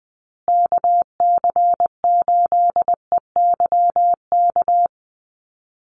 Morse code, heard as a keyed tone, about 19 wpm, 700 Hz. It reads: KC8EYK
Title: USA Call Signs Repeated 2x 20wpm